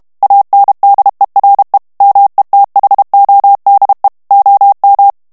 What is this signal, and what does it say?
Morse code, heard as a keyed tone, about 32 wpm, 800 Hz. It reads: ANDERE METHODE OM
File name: HOLLANDAIS texte clair 2 vitesse 30 mots minute